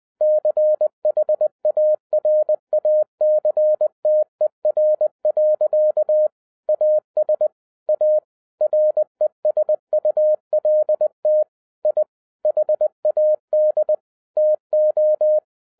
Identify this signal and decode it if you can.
Morse code, heard as a keyed tone, about 20 wpm, 610 Hz. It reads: CHARACTER. AS A RESULT I HAD TO
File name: 2011-10-mp3 / n13-TIPS FROM MY 34 YEARS OF EXPERIENCE 20wpm